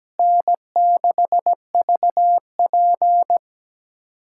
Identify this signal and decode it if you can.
Morse code, heard as a keyed tone, about 17 wpm, 700 Hz. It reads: N6VP
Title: USA Callsign DE Callsign - 4 Characters - 17wpm